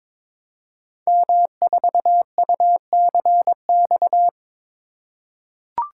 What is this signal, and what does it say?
Morse code, heard as a keyed tone, about 22 wpm, 700 Hz. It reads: M4UCX
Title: Code Groups - 5 Letter:Numbers - 22wpm